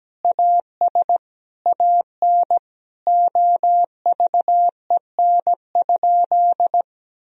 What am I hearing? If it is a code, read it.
Morse code, heard as a keyed tone, about 17 wpm, 700 Hz. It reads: AS AN OVEN?
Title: Sentences from Top 1000 Words - Review Part 4 - 17wpm